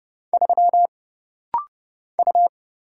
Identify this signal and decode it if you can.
Morse code, heard as a keyed tone, about 30 wpm, 700 Hz. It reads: 3 U
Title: Single Letter-Number Alt - Rapid-Fire 30wpm